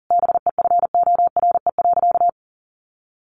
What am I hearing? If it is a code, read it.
Morse code, heard as a keyed tone, about 40 wpm, 700 Hz. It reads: BEFORE.
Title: Sentences from Top 1100 Words 40wpm